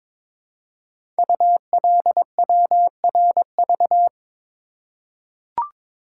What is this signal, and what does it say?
Morse code, heard as a keyed tone, about 22 wpm, 700 Hz. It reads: ULWRV
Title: Code Groups - 5 Letters - 22wpm